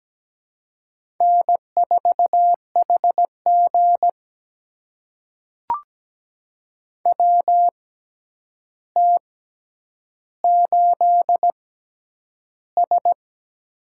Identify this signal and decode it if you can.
Morse code, heard as a keyed tone, about 17 wpm, spaced as for 9 wpm, 700 Hz. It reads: N4HG WT8S
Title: USA Call Signs - 4 Characters - 6x Character Spacing - 17wpm